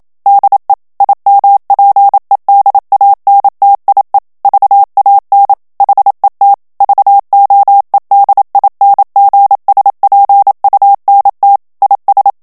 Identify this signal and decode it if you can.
Morse code, heard as a keyed tone, about 27 wpm, 800 Hz. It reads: DE IMPEDANTIE VAN HET VOEDINGSPUNT IS
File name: HOLLANDAIS texte clair 1 vitesse 26 mots minute